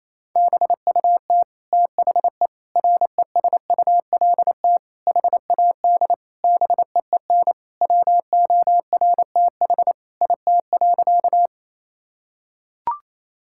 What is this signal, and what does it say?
Morse code, heard as a keyed tone, about 28 wpm, 700 Hz. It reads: BUT THE RESULT HAD BEEN WORTH IT.
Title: Sentences from Top 1000 Words - Review Part 5 - 28wpm